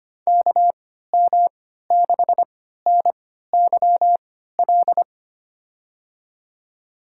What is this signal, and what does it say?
Morse code, heard as a keyed tone, about 25 wpm, spaced as for 14 wpm, 700 Hz. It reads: KM6NYL
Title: USA Call Signs - 6 Characters - 3x Character Spacing - 25wpm